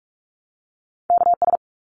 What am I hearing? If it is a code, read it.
Morse code, heard as a keyed tone, about 44 wpm, 700 Hz. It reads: KS
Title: ARRL Sections 45wpm